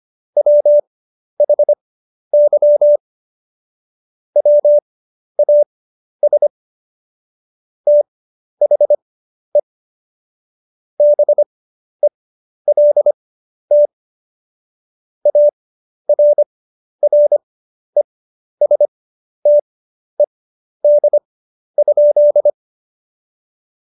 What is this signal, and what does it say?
Morse code, heard as a keyed tone, about 25 wpm, spaced as for 11 wpm, 590 Hz. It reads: WHY WAS THE BELT ARRESTED?
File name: Jokes-25-char-10-eff